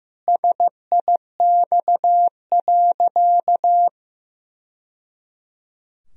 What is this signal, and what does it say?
Morse code, heard as a keyed tone, about 15 wpm, 700 Hz. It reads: SIX.
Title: Sentences from Top 1000 Words - Review Part 2 - 15wpm